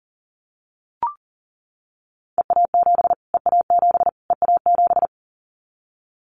Morse code, heard as a keyed tone, about 40 wpm, 700 Hz. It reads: EA7 EA7 EA7
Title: Callsign Prefixes for Top 25 DXCC - Repeated 3x - 40wpm